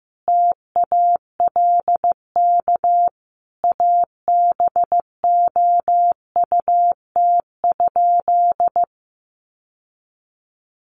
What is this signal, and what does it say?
Morse code, heard as a keyed tone, about 15 wpm, 700 Hz. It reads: TALK ABOUT?
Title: Sentences from Top 1100 Words 15wpm